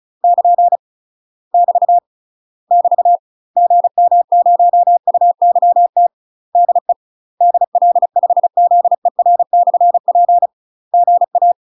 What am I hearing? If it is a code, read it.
Morse code, heard as a keyed tone, about 36 wpm, 700 Hz. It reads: <KN> <BT> <BT> GM0UYT DE DL5ZER/P GA
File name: UK MASTER QSOs TRAINING / 35WPM / UK MASTER QSOs 0026